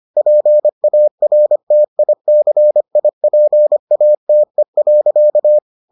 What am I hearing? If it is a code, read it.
Morse code, heard as a keyed tone, about 25 wpm, 600 Hz. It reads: PARTICIPATE.